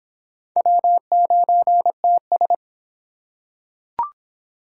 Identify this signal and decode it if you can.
Morse code, heard as a keyed tone, about 25 wpm, 700 Hz. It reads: W9TS